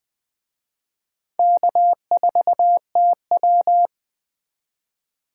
Morse code, heard as a keyed tone, about 20 wpm, 700 Hz. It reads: K4TW
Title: Speed Racing - USA 1x2 Call Signs 17wpm